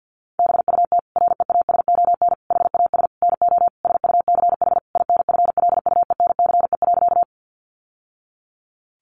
Morse code, heard as a keyed tone, about 50 wpm, 700 Hz. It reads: BUT REASON HAS NO SUCH INFLUENCE.
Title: Sentences from Top 1100 Words 50wpm